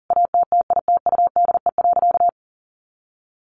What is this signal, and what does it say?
Morse code, heard as a keyed tone, about 40 wpm, 700 Hz. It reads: ATTITUDE.
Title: Sentences from Top 1300 Words 40wpm